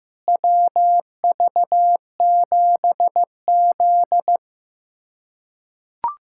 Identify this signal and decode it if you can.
Morse code, heard as a keyed tone, about 15 wpm, 700 Hz. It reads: WV7Z
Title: USA Callsign DE Callsign - 4 Characters - 15wpm